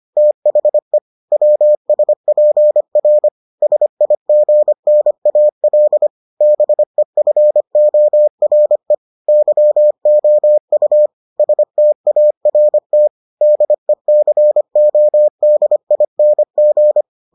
Morse code, heard as a keyed tone, about 25 wpm, 600 Hz. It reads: THE WSPR SIGNAL BEFORE YOU START DECODING